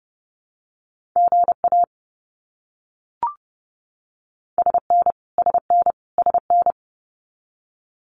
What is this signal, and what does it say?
Morse code, heard as a keyed tone, about 30 wpm, 700 Hz. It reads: GA SN SN SN
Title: Abbreviations Repeated 3x - 30wpm